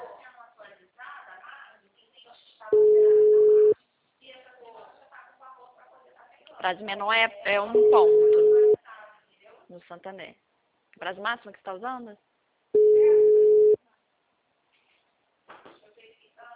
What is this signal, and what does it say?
Ringback tone (Europe, 425 Hz), with speech.